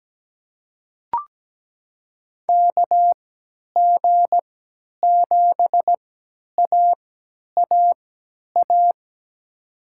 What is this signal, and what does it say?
Morse code, heard as a keyed tone, about 17 wpm, spaced as for 10 wpm, 700 Hz. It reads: KG7AAA
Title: USA Call Signs - 3x Character Spacing - 17wpm